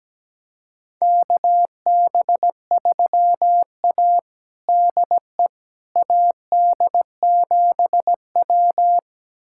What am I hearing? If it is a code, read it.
Morse code, heard as a keyed tone, about 17 wpm, 700 Hz. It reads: KB3A DE AD7W
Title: USA Callsign DE Callsign - 4 Characters - 17wpm